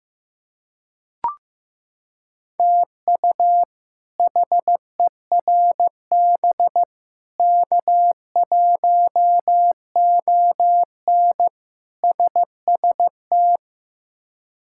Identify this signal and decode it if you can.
Morse code, heard as a keyed tone, about 15 wpm, 700 Hz. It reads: TU HERB K1ON SST
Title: K1USN SST QSO Practice - 15wpm